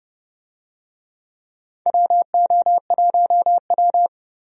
Morse code, heard as a keyed tone, about 30 wpm, 700 Hz. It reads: WO1W
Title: USA Call Signs - 4 Characters - 30wpm